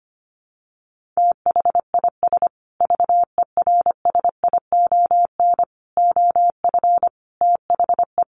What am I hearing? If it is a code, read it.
Morse code, heard as a keyed tone, about 25 wpm, 700 Hz. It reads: THIS VERSION OF THE
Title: Sentences from Top 1400 Words 25wpm